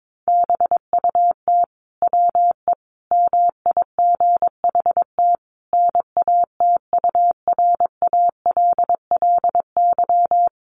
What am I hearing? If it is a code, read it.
Morse code, heard as a keyed tone, about 22 wpm, 700 Hz. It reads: BUT WE MIGHT NATURALLY
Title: Sentences from Top 1300 Words 22wpm